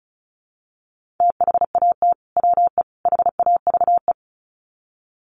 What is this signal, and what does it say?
Morse code, heard as a keyed tone, about 34 wpm, 700 Hz. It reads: THAT WE HAVE
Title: Sets of 3 Words 35wpm